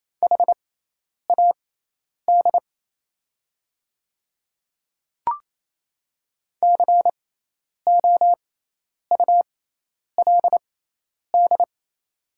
Morse code, heard as a keyed tone, about 28 wpm, spaced as for 10 wpm, 700 Hz. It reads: HAD COULD
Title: 6x Character Spacing - Top 100 Words - 28wpm